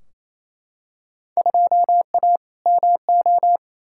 Morse code, heard as a keyed tone, about 28 wpm, 700 Hz. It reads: 2A MO
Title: Field Day Exchanges Encore - 28wpm